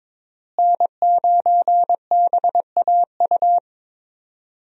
Morse code, heard as a keyed tone, about 22 wpm, 700 Hz. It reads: N9BAU
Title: USA Call Signs - 5 Characters - Repeated 3x - 22wpm